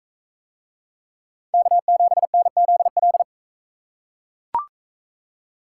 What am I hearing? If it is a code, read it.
Morse code, heard as a keyed tone, about 37 wpm, 700 Hz. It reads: K7NZL